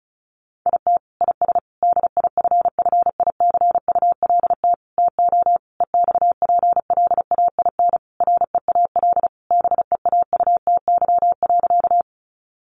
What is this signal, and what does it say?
Morse code, heard as a keyed tone, about 34 wpm, 700 Hz. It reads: IT IS DIFFICULT TO EXPLAIN REAL BEAUTY.
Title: Sentences from Top 1100 Words 35wpm